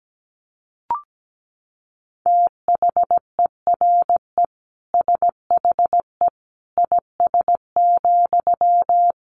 Morse code, heard as a keyed tone, about 17 wpm, 700 Hz. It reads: THERE SHE IS,
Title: Sentences from Top 900 Words 17wpm